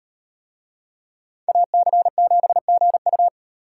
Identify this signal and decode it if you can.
Morse code, heard as a keyed tone, about 38 wpm, 700 Hz. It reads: AC7GU